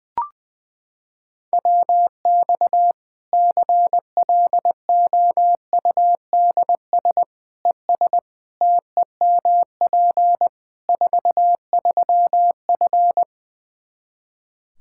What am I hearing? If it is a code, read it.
Morse code, heard as a keyed tone, about 21 wpm, 700 Hz. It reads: WX CLOUDS ES TEMP 43F
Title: K1USN SST QSO Practice 20wpm